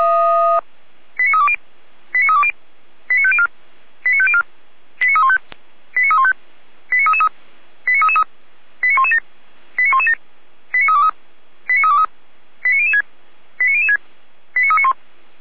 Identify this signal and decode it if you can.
Five-tone selective call (ZVEI), heard as quick runs of five tones, tones of about 80 ms. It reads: 89320 89320 89504 89504 89315 89315 89333 89333 89117 89117 89323 89323 89006 89006 89391